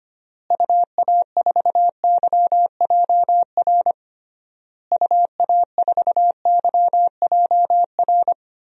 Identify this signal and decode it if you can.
Morse code, heard as a keyed tone, about 25 wpm, 700 Hz. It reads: UA4YJR UA4YJR
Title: DXCC-100 Call Signs - 6 Characters - Repeated 3x - 25wpm